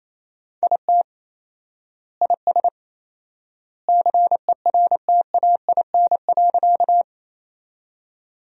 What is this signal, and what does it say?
Morse code, heard as a keyed tone, about 28 wpm, 700 Hz. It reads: IT IS CERTAIN.
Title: 4x Word-Spacing - Sentences from Top 500 Words - 28wpm